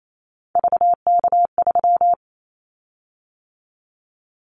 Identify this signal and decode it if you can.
Morse code, heard as a keyed tone, about 28 wpm, 700 Hz. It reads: VK3
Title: Callsign Prefixes for Top 100 DXCC - 28wpm